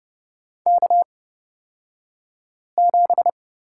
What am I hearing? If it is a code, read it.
Morse code, heard as a keyed tone, about 30 wpm, spaced as for 5 wpm, 700 Hz. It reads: K7
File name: Lesson 37d - Letter x - Two Characters 30wpm